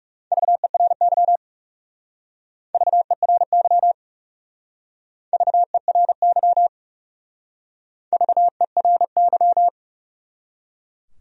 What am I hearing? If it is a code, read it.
Morse code, heard as a keyed tone, about 36 wpm, 700 Hz. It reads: VERY VERY VERY VERY
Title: Speed-Racing Top 200 Words 30wpm